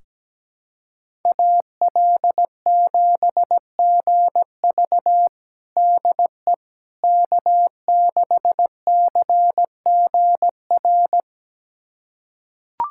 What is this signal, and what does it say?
Morse code, heard as a keyed tone, about 17 wpm, 700 Hz. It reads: AL7GV DE K6CGR
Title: USA Callsign DE Callsign - 5 Characters - 17wpm